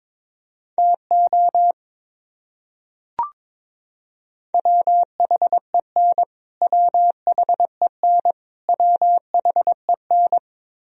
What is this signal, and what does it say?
Morse code, heard as a keyed tone, about 22 wpm, 700 Hz. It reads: TO WHEN WHEN WHEN
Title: Top 100 Words Repeated 22wpm